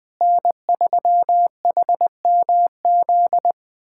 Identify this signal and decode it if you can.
Morse code, heard as a keyed tone, about 20 wpm, 700 Hz. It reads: N3HMZ
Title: USA Callsign DE Callsign - 5 Characters - 20wpm